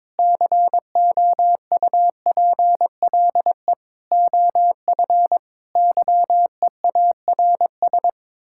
Morse code, heard as a keyed tone, about 22 wpm, 700 Hz. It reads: COUPLE OF YEARS
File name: practice / 22wpm / Build A Phrase - 5-Word Phrases - 22wpm